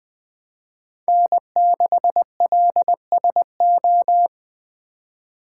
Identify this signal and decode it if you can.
Morse code, heard as a keyed tone, about 20 wpm, 700 Hz. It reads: N6LSO
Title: Speed Racing - USA 1x3 Call Signs 17wpm